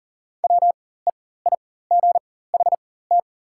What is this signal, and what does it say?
Morse code, heard as a keyed tone, about 40 wpm, spaced as for 19 wpm, 690 Hz. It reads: WEIGHT
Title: 4x Character Spacing - Top 500 Words - 40wpm